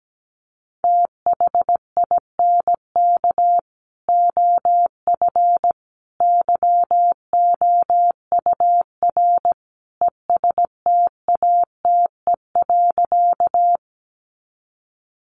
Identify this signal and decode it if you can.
Morse code, heard as a keyed tone, about 17 wpm, 700 Hz. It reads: THINK OF YOUR ESTATE.
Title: Sentences from Top 1200 Words 17wpm